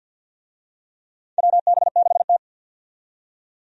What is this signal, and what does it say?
Morse code, heard as a keyed tone, about 52 wpm, 700 Hz. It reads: WB6T
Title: USA Call Signs - 4 Characters - 50wpm